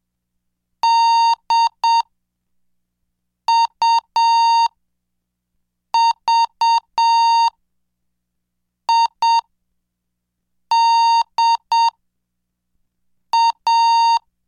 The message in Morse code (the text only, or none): DUVIDA